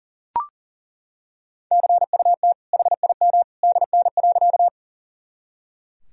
Morse code, heard as a keyed tone, about 40 wpm, 700 Hz. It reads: CUT HIM DN.
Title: Sentences from Top 300 Words and Abbreviations 40wpm